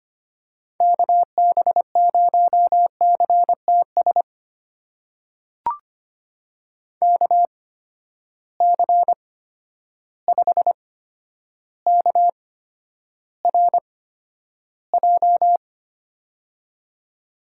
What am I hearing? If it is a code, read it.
Morse code, heard as a keyed tone, about 25 wpm, spaced as for 11 wpm, 700 Hz. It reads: KB0CTS KC5KRJ
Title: USA Call Signs - 6 Characters - 8x Character Spacing - 25wpm